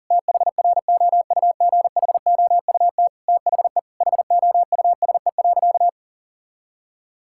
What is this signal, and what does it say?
Morse code, heard as a keyed tone, about 40 wpm, 700 Hz. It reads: THROUGHOUT THE HOUSE.
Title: Sentences from Top 1000 Words - Review Part 5 - 40wpm